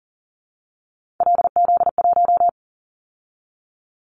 Morse code, heard as a keyed tone, about 42 wpm, 700 Hz. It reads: LZ1
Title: Callsign Prefixes for Top 25 DXCC - 40wpm